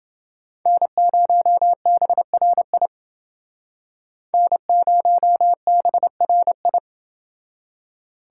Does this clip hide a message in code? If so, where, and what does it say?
Yes: Morse code from 0.7 to 6.8 s, heard as a keyed tone, about 28 wpm, 700 Hz. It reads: N0BRI N0BRI